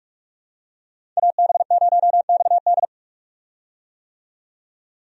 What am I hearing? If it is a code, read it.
Morse code, heard as a keyed tone, about 47 wpm, 700 Hz. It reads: AB0XD
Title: USA Call Signs - 5 Characters - 45wpm